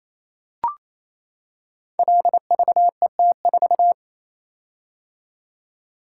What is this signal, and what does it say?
Morse code, heard as a keyed tone, about 27 wpm, 700 Hz. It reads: LVET4
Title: Code Groups - 5 Letter:Numbers - 28wpm